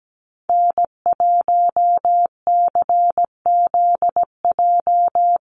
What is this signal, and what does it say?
Morse code, heard as a keyed tone, about 17 wpm, 700 Hz. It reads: N1CZJ
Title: USA Call Signs Repeated 3x 17wpm